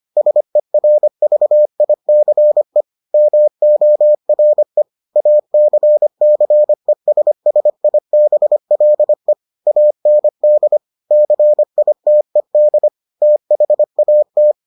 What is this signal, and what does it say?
Morse code, heard as a keyed tone, about 25 wpm, 600 Hz. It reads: SERVICE MORE ACCESSIBLE AND CITED THAT